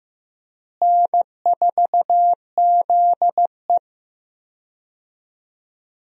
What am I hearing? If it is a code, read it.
Morse code, heard as a keyed tone, about 15 wpm, 700 Hz. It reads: N4ZE
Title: USA Call Signs - 4 Characters - Repeated 3x - 15wpm